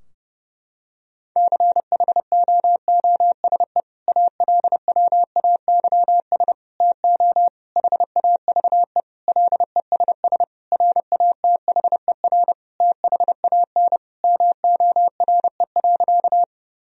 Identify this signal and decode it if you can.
Morse code, heard as a keyed tone, about 32 wpm, 700 Hz. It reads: CHOOSE ALWAYS TO HAVE LESS RATHER THAN MORE.
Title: Sentences from Top 1000 Words - Review Part 4 - 30wpm